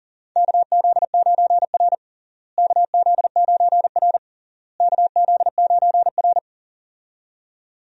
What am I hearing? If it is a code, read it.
Morse code, heard as a keyed tone, about 37 wpm, 700 Hz. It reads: KZ9R KZ9R KZ9R
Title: USA Call Signs - 4 Characters - Repeated 3x - 40wpm